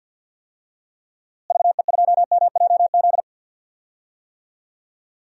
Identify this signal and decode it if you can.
Morse code, heard as a keyed tone, about 45 wpm, 700 Hz. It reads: VE2MJZ
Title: DXCC-100 Call Signs - 6 Characters - Repeated 2x - 50wpm